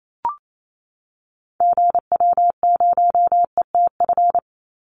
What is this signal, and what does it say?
Morse code, heard as a keyed tone, about 28 wpm, 700 Hz. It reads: GW0ETF
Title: Call Signs - 28wpm